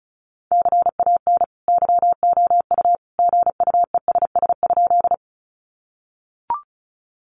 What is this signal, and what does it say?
Morse code, heard as a keyed tone, about 34 wpm, 700 Hz. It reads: CAN YOU GUESS?
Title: Sentences from Top 1500 Words 35wpm